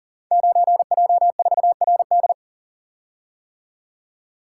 Morse code, heard as a keyed tone, about 39 wpm, 700 Hz. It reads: 9J4RD
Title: Code Groups - 5 Letter:Numbers - 40wpm